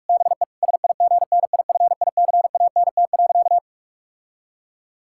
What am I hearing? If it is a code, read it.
Morse code, heard as a keyed tone, about 43 wpm, 700 Hz. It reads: BE SIGNIFICANT.
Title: Sentences from Top 1000 Words - Review Part 6 - 45wpm